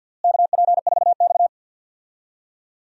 Morse code, heard as a keyed tone, about 50 wpm, 700 Hz. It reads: KP4X
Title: DXCC-100 Call Signs - 4 Characters - Repeated 3x - 50wpm